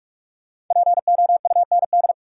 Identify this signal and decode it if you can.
Morse code, heard as a keyed tone, about 43 wpm, 700 Hz. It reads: POUND
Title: Sentences from Top 1000 Words - Review Part 2 - New Word Drill - 45wpm